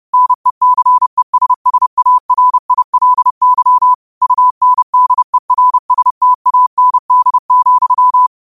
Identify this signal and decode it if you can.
Morse code, heard as a keyed tone, about 30 wpm, 1000 Hz. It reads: NECESSARILY UNDERSTAND,